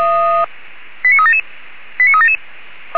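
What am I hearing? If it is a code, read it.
Five-tone selective call (ZVEI), heard as quick runs of five tones, tones of about 70 ms. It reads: 89377 89377
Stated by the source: ZVEI five-tone call 89376, received by SDR.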